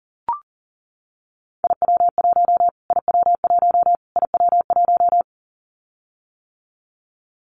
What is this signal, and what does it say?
Morse code, heard as a keyed tone, about 40 wpm, 700 Hz. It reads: IW1 IW1 IW1
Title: Callsign Prefixes for Top 25 DXCC - Repeated 3x - 40wpm